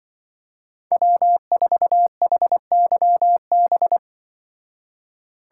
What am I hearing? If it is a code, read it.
Morse code, heard as a keyed tone, about 24 wpm, 700 Hz. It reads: W4HYB
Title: Speed Racing - USA 1x3 Call Signs 20wpm